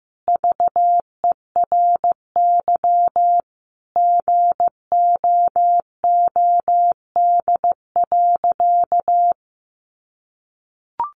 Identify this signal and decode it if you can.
Morse code, heard as a keyed tone, about 15 wpm, 700 Hz. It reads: VERY GOOD.
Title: Sentences from Top 1500 Words 15wpm